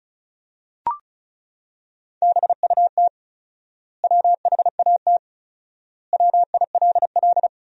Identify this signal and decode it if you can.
Morse code, heard as a keyed tone, about 34 wpm, 700 Hz. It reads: BUT WHAT WILL
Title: 4x Word-Spacing - Sentences from Top 100 Words - 35wpm